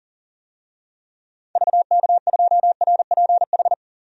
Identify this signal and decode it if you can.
Morse code, heard as a keyed tone, about 37 wpm, 700 Hz. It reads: VK2RPH
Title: DXCC-100 Call Signs - 6 Characters - 40wpm